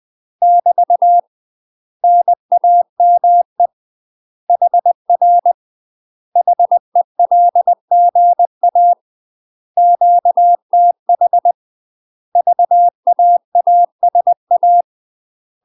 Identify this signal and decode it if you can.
Morse code, heard as a keyed tone, about 20 wpm, 700 Hz. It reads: <BT> NAME HR HELGA QTH VAASA
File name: UK MASTER QSOs TRAINING / 20WPM / UK MASTER QSOs 0045